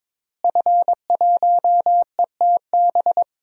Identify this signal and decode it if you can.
Morse code, heard as a keyed tone, about 22 wpm, 700 Hz. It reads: F1ETB
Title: DXCC-100 Call Signs - 5 Characters - Repeated 3x - 22wpm